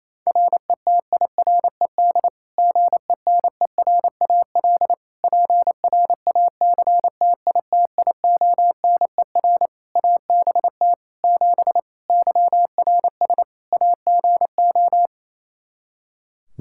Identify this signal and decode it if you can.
Morse code, heard as a keyed tone, about 28 wpm, 700 Hz. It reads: RETIRED GENERAL PRACTITIONER ABT 7 YRS AGO
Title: Practice QSO - Exchange 3 - More Info - 28wpm